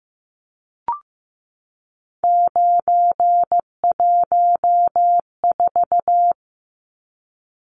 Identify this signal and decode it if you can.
Morse code, heard as a keyed tone, about 15 wpm, 700 Hz. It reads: 914
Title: Numbers - 3-Digit 15wpm